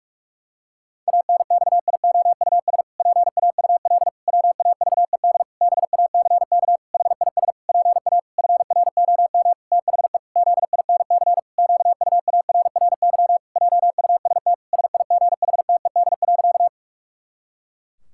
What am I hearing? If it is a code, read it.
Morse code, heard as a keyed tone, about 41 wpm, 700 Hz. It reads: ANXIOUS PAUL WAVED BACK HIS PA FROM THE ZINC QUARRY JUST SIGHTED.
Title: Pangram Challenge v2 - 45wpm